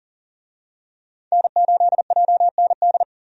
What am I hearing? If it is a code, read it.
Morse code, heard as a keyed tone, about 40 wpm, 700 Hz. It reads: N8JND
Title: USA Call Signs - 5 Characters - 40wpm